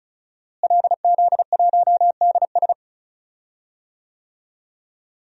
Morse code, heard as a keyed tone, about 34 wpm, 700 Hz. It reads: LZ1DS
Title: DXCC-100 Call Signs - 5 Characters - Repeated 2x - 35wpm